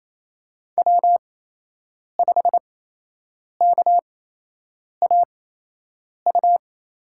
Morse code, heard as a keyed tone, about 27 wpm, spaced as for 8 wpm, 700 Hz. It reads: W5KAU
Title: USA Call Signs - 5 Characters - 8x Character Spacing - 28wpm